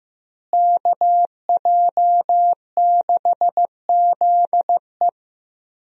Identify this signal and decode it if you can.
Morse code, heard as a keyed tone, about 15 wpm, 700 Hz. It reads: KJ6ZE
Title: USA Call Signs - 5 Characters - Repeated 3x - 15wpm